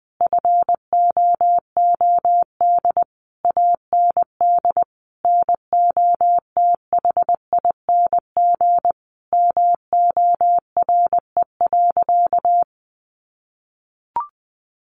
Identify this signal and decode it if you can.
Morse code, heard as a keyed tone, about 20 wpm, 700 Hz. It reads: FOOD AND NOTHING MORE.
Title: Sentences from Top 800 Words 20wpm